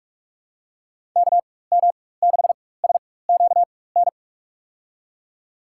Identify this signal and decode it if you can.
Morse code, heard as a keyed tone, about 46 wpm, spaced as for 21 wpm, 700 Hz. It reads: KM6SQN
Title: USA Call Signs - 6 Characters - 4x Character Spacing - 45wpm